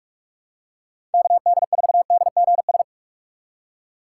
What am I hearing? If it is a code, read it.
Morse code, heard as a keyed tone, about 46 wpm, 700 Hz. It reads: KD4DGS
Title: USA Call Signs - 6 Characters - 3x Character Spacing - 45wpm